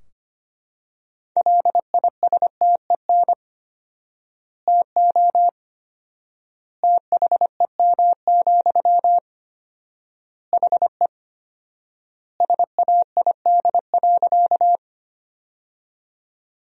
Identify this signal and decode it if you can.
Morse code, heard as a keyed tone, about 25 wpm, 700 Hz. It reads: LISTEN TO THEM, HE SAID.
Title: 4x Word-Spacing - Sentences from Top 400 Words - 25wpm